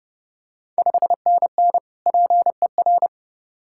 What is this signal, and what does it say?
Morse code, heard as a keyed tone, about 30 wpm, 700 Hz. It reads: 5NN PER
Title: Ontario QSO Party Exchange 30wpm